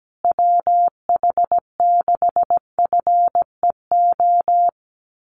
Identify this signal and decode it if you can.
Morse code, heard as a keyed tone, about 17 wpm, 700 Hz. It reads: WH6FEO
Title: USA Call Signs Repeated 3x 17wpm